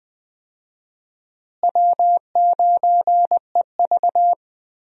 Morse code, heard as a keyed tone, about 20 wpm, 700 Hz. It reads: W9EV
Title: USA Call Signs - 4 Characters - Repeated 2x - 20wpm